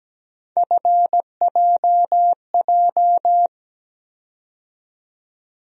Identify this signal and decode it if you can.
Morse code, heard as a keyed tone, about 17 wpm, 700 Hz. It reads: FJJ
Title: USA Call Sign Postfixes - 3 Letters 17wpm